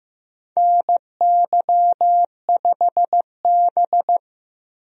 Morse code, heard as a keyed tone, about 15 wpm, 700 Hz. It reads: NY5B
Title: USA Call Signs - 4 Characters - 15wpm